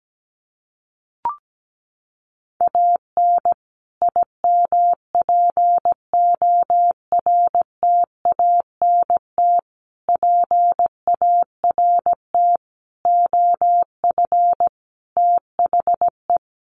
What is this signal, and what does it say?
Morse code, heard as a keyed tone, about 17 wpm, 700 Hz. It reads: AN IMPORTANT PART OF THE
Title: Sets of 5 Words Encore 17wpm